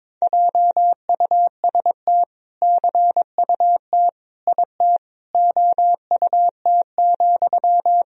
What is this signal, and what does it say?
Morse code, heard as a keyed tone, about 21 wpm, 700 Hz. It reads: JUST CUT IT OUT,
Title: Sentences from Top 300 Words and Abbreviations 22wpm